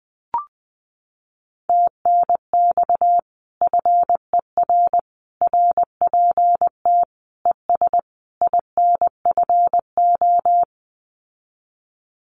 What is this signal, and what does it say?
Morse code, heard as a keyed tone, about 20 wpm, 700 Hz. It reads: TNX FER RPT ES INFO
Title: Common QSO Phrases Encore 20wpm